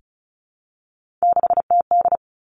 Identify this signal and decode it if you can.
Morse code, heard as a keyed tone, about 36 wpm, 700 Hz. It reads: 6TD
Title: Three Challenging Letter-Number Combinations 35wpm